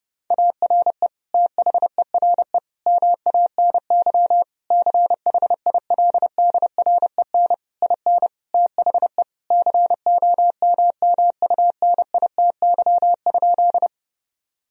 Morse code, heard as a keyed tone, about 30 wpm, 700 Hz. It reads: ARE THERE MANY CHILDREN IN THE COMMUNITY?
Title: Sentences from Top 1000 Words - Review Part 4 - 30wpm